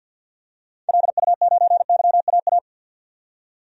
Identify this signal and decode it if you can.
Morse code, heard as a keyed tone, about 51 wpm, 700 Hz. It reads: LU9YAA
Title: DXCC-100 Call Signs - 6 Characters - 50wpm